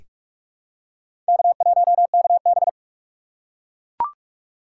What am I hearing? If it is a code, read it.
Morse code, heard as a keyed tone, about 43 wpm, 700 Hz. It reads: K1KB
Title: USA Call Signs - 4 Characters - 6x Character Spacing - 45wpm